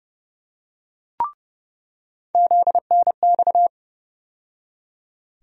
Morse code, heard as a keyed tone, about 32 wpm, 700 Hz. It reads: ZNX
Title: USA Call Sign Postfixes - 3 Letters 30wpm